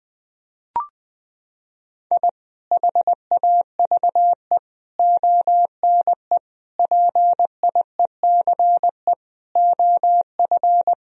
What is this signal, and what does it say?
Morse code, heard as a keyed tone, about 20 wpm, 700 Hz. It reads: I HAVE ONE PIECE OF
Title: Sentences from Top 1000 Words - Review Part 2 - 20wpm